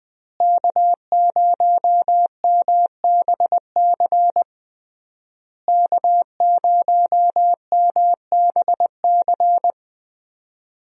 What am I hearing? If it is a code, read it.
Morse code, heard as a keyed tone, about 21 wpm, 700 Hz. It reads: K0MBC K0MBC
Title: USA Call Signs - 5 Characters - Repeated 3x - 20wpm